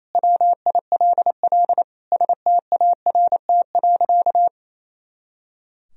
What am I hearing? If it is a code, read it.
Morse code, heard as a keyed tone, about 28 wpm, 700 Hz. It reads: WILL START.
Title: Sentences from Top 300 Words and Abbreviations 28wpm